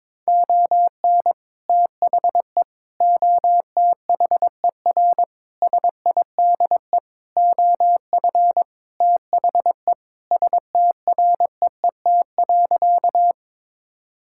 Morse code, heard as a keyed tone, about 22 wpm, 700 Hz. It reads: ON THE OTHER SIDE OF THE STREET.
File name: practice / 22wpm / Sentences with 2-Word Phrases - 22wpm